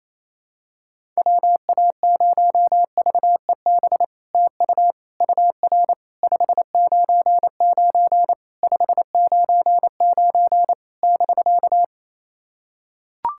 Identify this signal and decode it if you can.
Morse code, heard as a keyed tone, about 28 wpm, 700 Hz. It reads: WA0VEB TU UR 599 599 <BK>
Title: SOTA QSO Practice 28wpm